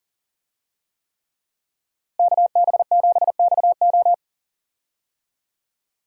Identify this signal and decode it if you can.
Morse code, heard as a keyed tone, about 40 wpm, 700 Hz. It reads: KB7XO
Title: USA Call Signs - 5 Characters - Repeated 3x - 40wpm